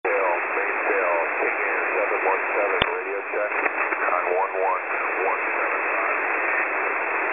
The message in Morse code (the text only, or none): none